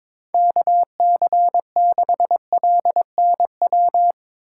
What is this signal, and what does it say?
Morse code, heard as a keyed tone, about 22 wpm, 700 Hz. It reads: KC6LNW
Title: USA Call Signs - 6 Characters - 8x Character Spacing - 22wpm